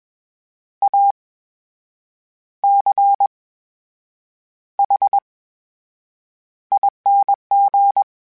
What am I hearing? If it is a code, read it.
Morse code, heard as a keyed tone, about 21 wpm, spaced as for 7 wpm, 800 Hz. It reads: ACHING
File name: Special INGs-mp3s / Special INGs-mp3s / VerySlowToFaster Ing @20